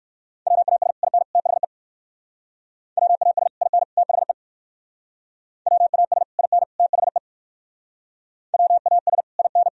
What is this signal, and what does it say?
Morse code, heard as a keyed tone, about 62 wpm, 700 Hz. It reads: WAS IN THE WAS IN THE WAS IN THE WAS IN